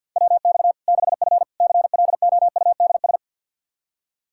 Morse code, heard as a keyed tone, about 46 wpm, 700 Hz. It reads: WX 6F CLOUDS
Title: Practice QSO - Exchange 2 - Info - 50wpm